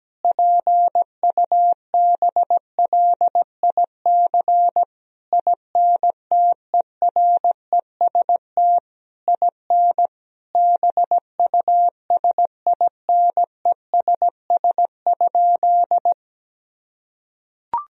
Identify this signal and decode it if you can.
Morse code, heard as a keyed tone, about 17 wpm, 700 Hz. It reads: PUBLIC INTEREST IN BUSINESS?
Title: Sentences from Top 1000 Words - Review Part 3 - 17wpm